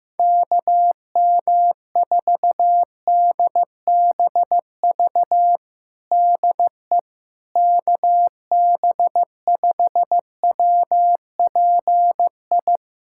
Morse code, heard as a keyed tone, about 15 wpm, 700 Hz. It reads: KM4DBV DE KB5WPI
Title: USA Callsign DE Callsign - 6 Characters - 15wpm